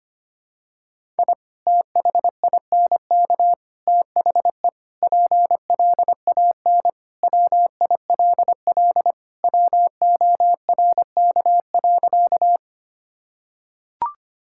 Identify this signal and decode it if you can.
Morse code, heard as a keyed tone, about 25 wpm, 700 Hz. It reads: I THINK THE PLAN WILL WORK.
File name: practice / 25wpm / Sentences with 3-Word Phrases - 25wpm